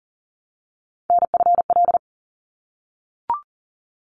Morse code, heard as a keyed tone, about 40 wpm, 700 Hz. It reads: NFL
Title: ARRL Sections 40wpm